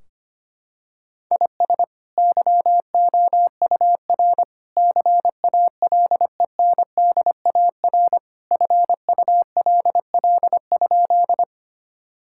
Morse code, heard as a keyed tone, about 25 wpm, 700 Hz. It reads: IS YOUR CALENDAR FULL?
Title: Sentences from Top 1600 Words 25wpm